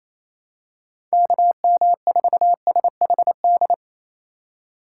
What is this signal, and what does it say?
Morse code, heard as a keyed tone, about 27 wpm, 700 Hz. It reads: KM4SHD
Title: USA Call Signs - 2x Character Spacing - 28wpm